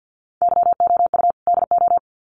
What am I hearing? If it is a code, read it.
Morse code, heard as a keyed tone, about 48 wpm, 700 Hz. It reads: YOU DO
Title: Sets of 2 Words 50wpm